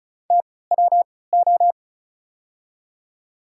Morse code, heard as a keyed tone, about 36 wpm, spaced as for 20 wpm, 700 Hz. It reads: TWO
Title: 3x Character Spacing - Top 300 Words - 35wpm